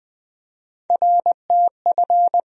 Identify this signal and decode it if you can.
Morse code, heard as a keyed tone, about 20 wpm, 700 Hz. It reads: RTF
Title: USA Call Sign Postfixes - 3 Letters 20wpm